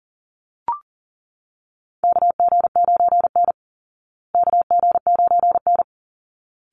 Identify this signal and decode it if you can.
Morse code, heard as a keyed tone, about 40 wpm, 700 Hz. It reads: KG9N KG9N
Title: Call Signs Repeated 3x - 40wpm